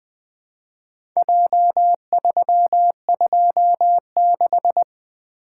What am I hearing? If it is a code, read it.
Morse code, heard as a keyed tone, about 20 wpm, 700 Hz. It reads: J326
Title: Four Letter-Number Combinations - 20wpm